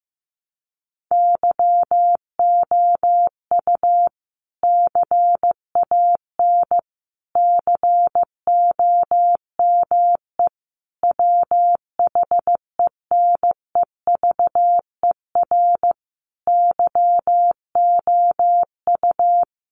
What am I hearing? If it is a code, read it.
Morse code, heard as a keyed tone, about 15 wpm, 700 Hz. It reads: YOU CAN COME WHENEVER YOU
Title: Sentences from Top 1200 Words 15wpm